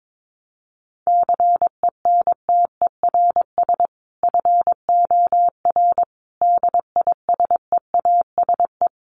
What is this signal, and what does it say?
Morse code, heard as a keyed tone, about 22 wpm, 700 Hz. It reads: CENTERS FOR DISEASE
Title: Sets of 5 Words Encore2 22wpm